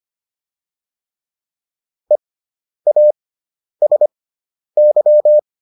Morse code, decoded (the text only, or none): EASY